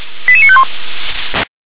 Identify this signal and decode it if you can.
Five-tone selective call (ZVEI), heard as quick runs of five tones, tones of about 70 ms. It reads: 88951